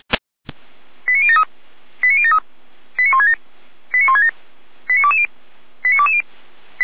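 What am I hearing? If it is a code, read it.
Five-tone selective call (ZVEI), heard as quick runs of five tones, tones of about 70 ms. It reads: 89073 89073 89167 89167 89220 89220
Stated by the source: ZVEI five-tone selective call received by SDR.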